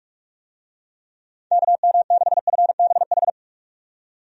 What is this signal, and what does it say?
Morse code, heard as a keyed tone, about 45 wpm, 700 Hz. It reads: KM6FBH
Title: USA Call Signs - 6 Characters - 45wpm